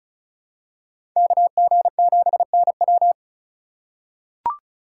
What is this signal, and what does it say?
Morse code, heard as a keyed tone, about 36 wpm, 700 Hz. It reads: KG7NW
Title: USA Call Signs - 2x Character Spacing - 35wpm